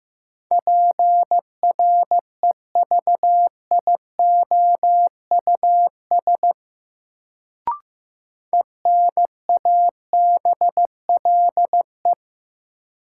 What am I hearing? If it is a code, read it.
Morse code, heard as a keyed tone, about 15 wpm, 700 Hz. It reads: PREVIOUS ENABLE
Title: Sentences from Top 1500 Words - New Word Drill Alt - 15wpm